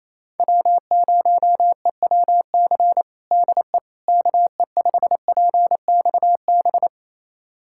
Morse code, heard as a keyed tone, about 28 wpm, 700 Hz. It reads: W0EWC DE KE5PXB
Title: USA Callsign DE Callsign 28wpm